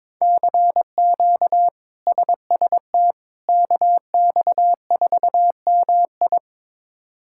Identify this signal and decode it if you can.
Morse code, heard as a keyed tone, about 22 wpm, 700 Hz. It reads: CQ SST KX4MI
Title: K1USN SST QSO Practice - 22wpm